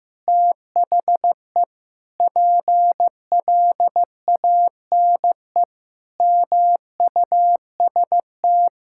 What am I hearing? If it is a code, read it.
Morse code, heard as a keyed tone, about 15 wpm, 700 Hz. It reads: THE PLANE MUST
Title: Sentences from Top 1000 Words - Review Part 6 - 15wpm